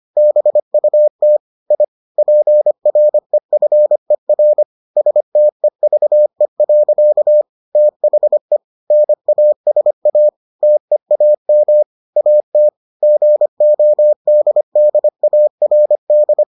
Morse code, heard as a keyed tone, about 25 wpm, 600 Hz. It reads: BUT I PREFER STEVE. THE NASA TEAM AT GODDARD